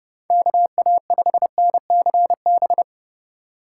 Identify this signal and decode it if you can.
Morse code, heard as a keyed tone, about 30 wpm, 700 Hz. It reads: KA5NCB
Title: USA Call Signs - 6 Characters - 2x Character Spacing - 30wpm